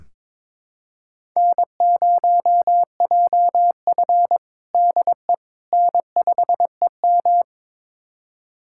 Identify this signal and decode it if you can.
Morse code, heard as a keyed tone, about 22 wpm, 700 Hz. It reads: N0JF DE N5EM
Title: USA Callsign DE Callsign - 4 Characters - 22wpm